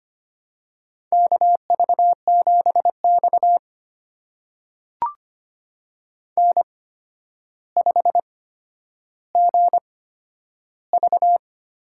Morse code, heard as a keyed tone, about 25 wpm, spaced as for 11 wpm, 700 Hz. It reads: KV7X N5GV
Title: USA Call Signs - 4 Characters - 8x Character Spacing - 25wpm